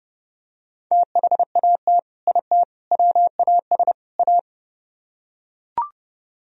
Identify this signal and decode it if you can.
Morse code, heard as a keyed tone, about 30 wpm, 700 Hz. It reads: THAT IT WAS A